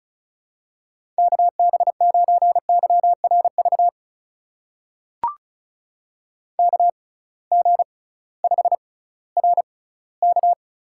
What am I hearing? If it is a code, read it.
Morse code, heard as a keyed tone, about 36 wpm, spaced as for 19 wpm, 700 Hz. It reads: KB9YRV KG5RK
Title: USA Call Signs - 6 Characters - 6x Character Spacing - 35wpm